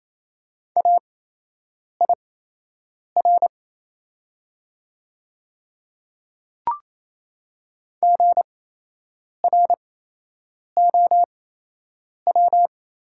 Morse code, heard as a keyed tone, about 28 wpm, spaced as for 8 wpm, 700 Hz. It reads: AIR GROW
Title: 8x Character Spacing - Top 500 Words - 28wpm